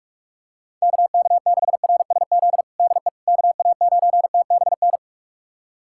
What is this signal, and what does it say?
Morse code, heard as a keyed tone, about 46 wpm, 700 Hz. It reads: KK6RSZ DE KA9TBN
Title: USA Callsign DE Callsign 45wpm